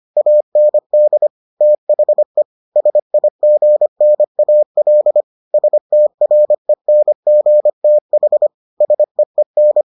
Morse code, heard as a keyed tone, about 25 wpm, 600 Hz. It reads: AND THE SIGNAL STRENGTH SEEN